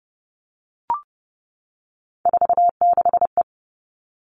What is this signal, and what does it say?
Morse code, heard as a keyed tone, about 30 wpm, 700 Hz. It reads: 46E
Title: All Cut Numbers - 3-Digit 30wpm